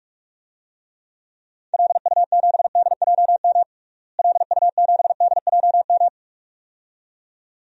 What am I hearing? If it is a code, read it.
Morse code, heard as a keyed tone, about 46 wpm, 700 Hz. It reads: LU7DJM LU7DJM
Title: DXCC-100 Call Signs - 6 Characters - Repeated 2x - 45wpm